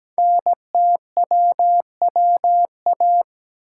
Morse code, heard as a keyed tone, about 17 wpm, 700 Hz. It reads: NTWWA
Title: Code Groups - 5 Letters - 17wpm